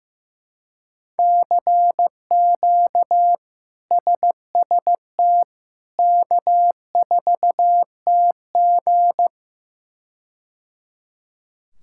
Morse code, heard as a keyed tone, about 16 wpm, 700 Hz. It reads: CQ SST K4TG
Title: K1USN SST QSO Practice - 15wpm